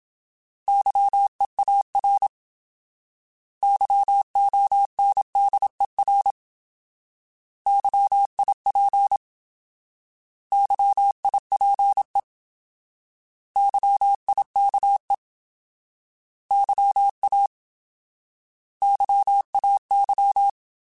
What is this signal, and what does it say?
Morse code, heard as a keyed tone, about 27 wpm, 800 Hz. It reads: YEAR YONDER YIP YIPE YIKE YA YAY
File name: Alphabetical / Alphabetical mp3s / Ys @25